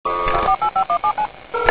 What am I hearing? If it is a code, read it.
DTMF, heard as telephone touch tones, tones of about 90 ms. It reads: 48517